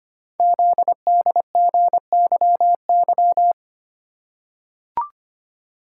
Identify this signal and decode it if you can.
Morse code, heard as a keyed tone, about 25 wpm, 700 Hz. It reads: ZDGYY
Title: Code Groups - 5 Letters - 25wpm